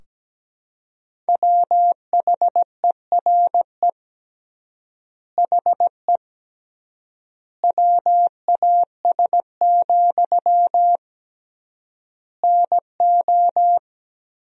Morse code, heard as a keyed tone, about 17 wpm, 700 Hz. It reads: WHERE HE WAS, NO
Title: 3x Word-Spacing - Sentences from Top 400 Words - 17wpm